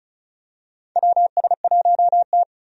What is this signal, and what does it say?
Morse code, heard as a keyed tone, about 34 wpm, 700 Hz. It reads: WS1T
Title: USA Call Signs - 4 Characters - 35wpm